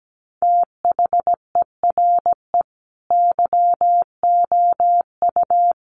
Morse code, heard as a keyed tone, about 17 wpm, 700 Hz. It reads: THERE YOU
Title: Sentences from Top 100 Words 17wpm